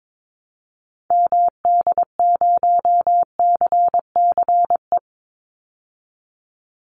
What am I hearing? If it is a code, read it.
Morse code, heard as a keyed tone, about 22 wpm, 700 Hz. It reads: MD0CCE
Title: Call Signs Repeated 2x - 22wpm